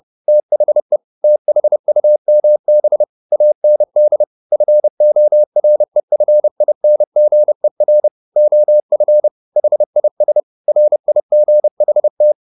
Morse code, heard as a keyed tone, about 30 wpm, 600 Hz. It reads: THE THUMB AND FOREFINGER OF HIS RIGHT
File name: the old man and the sea / the old man and the sea 1 30wpm